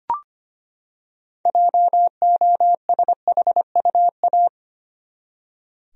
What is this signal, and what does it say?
Morse code, heard as a keyed tone, about 25 wpm, 700 Hz. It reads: JOSHUA
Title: USA Common Names - 25wpm